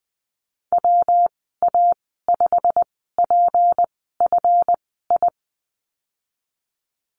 Morse code, heard as a keyed tone, about 19 wpm, spaced as for 14 wpm, 700 Hz. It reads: WA5PFI
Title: 173 Lesson 28g - Number 9 - Call Signs 20wpm